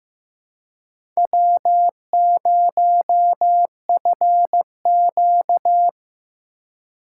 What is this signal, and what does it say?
Morse code, heard as a keyed tone, about 15 wpm, 700 Hz. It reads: W0FQ
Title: USA Call Signs - 4 Characters - 3x Character Spacing - 15wpm